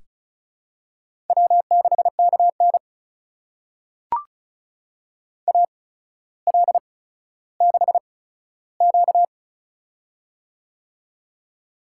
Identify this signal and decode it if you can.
Morse code, heard as a keyed tone, about 36 wpm, spaced as for 15 wpm, 700 Hz. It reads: W6KN AL6Q
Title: USA Call Signs - 4 Characters - 8x Character Spacing - 35wpm